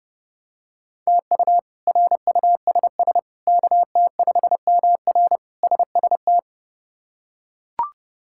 Morse code, heard as a keyed tone, about 30 wpm, 700 Hz. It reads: TU RUSS KT5MR SST
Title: K1USN SST QSO Practice - 30wpm